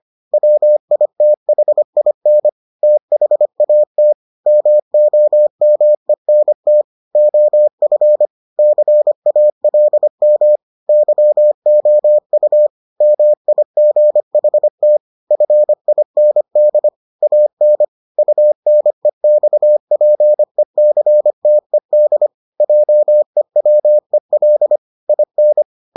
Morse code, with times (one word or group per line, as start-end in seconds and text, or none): 0.3-2.5 WITHIN
2.8-4.1 THAT
4.5-6.8 MOMENT
7.2-8.3 OF
8.6-10.6 CALM
10.9-12.7 YOU
13.0-15.0 MIGHT
15.3-16.9 FIND
17.2-17.8 AN
18.2-22.3 UNEXPECTED
22.6-24.8 JEWEL
25.1-25.6 IN